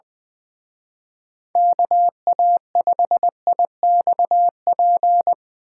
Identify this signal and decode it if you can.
Morse code, heard as a keyed tone, about 23 wpm, 700 Hz. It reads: KA5IXP
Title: Speed Racing - USA 2x3 Call Signs 20wpm